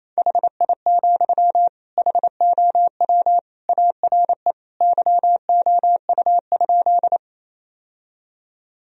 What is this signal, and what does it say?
Morse code, heard as a keyed tone, about 28 wpm, 700 Hz. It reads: HI, HOW ARE YOU?
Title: Sentences from Top 1000 Words - Review Part 3 - 28wpm